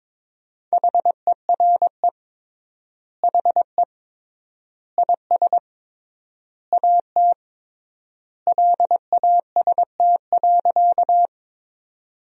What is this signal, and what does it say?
Morse code, heard as a keyed tone, about 22 wpm, 700 Hz. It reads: HERE HE IS AT LAST.
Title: 3x Word-Spacing - Sentences from Top 300 Words - 22wpm